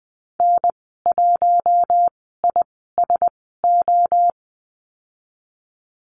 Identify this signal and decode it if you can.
Morse code, heard as a keyed tone, about 20 wpm, spaced as for 14 wpm, 700 Hz. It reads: N1ISO
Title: 042 Lesson 8g - Number 1 - Call Signs 20wpm